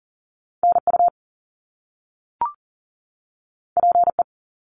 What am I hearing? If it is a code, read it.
Morse code, heard as a keyed tone, about 38 wpm, 700 Hz. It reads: NU PE
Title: Canadian Provinces and Territories 40wpm